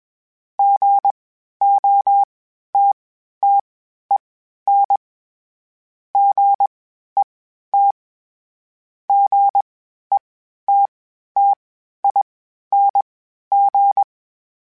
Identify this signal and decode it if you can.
Morse code, heard as a keyed tone, about 21 wpm, spaced as for 12 wpm, 800 Hz. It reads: GOTTEN GET GETTING
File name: Short Runs / Short Runs-mp3s / Gotten @20